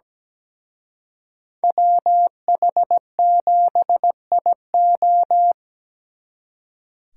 Morse code, heard as a keyed tone, about 20 wpm, 700 Hz. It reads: WH7IO
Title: Speed Racing - USA 2x2 Call Signs 17wpm